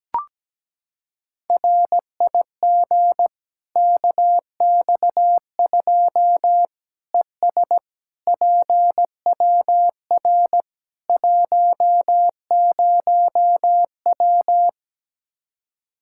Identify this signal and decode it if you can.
Morse code, heard as a keyed tone, about 17 wpm, 700 Hz. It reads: RIG KX2 ES PWR 10W
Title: K1USN SST QSO Practice 17wpm